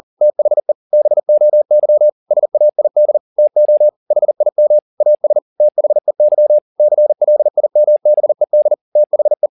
Morse code, heard as a keyed tone, about 40 wpm, 600 Hz. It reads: THE BOY SAID TO HIM AS THEY CLIMBED THE